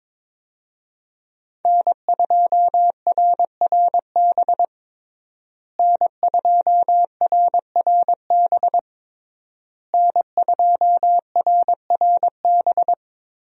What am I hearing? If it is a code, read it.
Morse code, heard as a keyed tone, about 21 wpm, 700 Hz. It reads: N2RRB N2RRB N2RRB
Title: USA Call Signs - 5 Characters - Repeated 3x - 22wpm